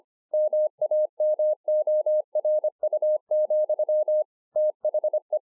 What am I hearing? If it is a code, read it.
Morse code, heard as a keyed tone, about 26 wpm, 610 Hz. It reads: MAMORU, THE
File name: 2011-04-mp3 / THE 3.11 3 25WPM